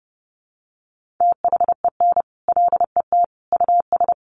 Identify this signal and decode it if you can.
Morse code, heard as a keyed tone, about 30 wpm, 700 Hz. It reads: THEN LET US
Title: Sentences from Top 1500 Words 30wpm